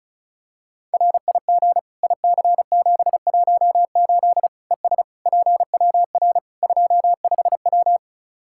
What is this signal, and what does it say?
Morse code, heard as a keyed tone, about 36 wpm, 700 Hz. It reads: RIG IC718 ES PWR 25W
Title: K1USN SST QSO Practice 35wpm